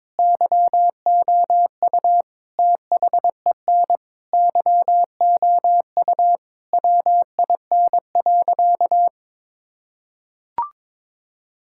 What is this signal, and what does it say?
Morse code, heard as a keyed tone, about 22 wpm, 700 Hz. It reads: YOU THEN YOU WIN.
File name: practice / 22wpm / Sentences with 8-Word Phrases - 22wpm